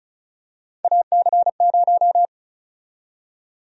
Morse code, heard as a keyed tone, about 36 wpm, 700 Hz. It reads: AC0
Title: USA Call Sign Prefixes - 35wpm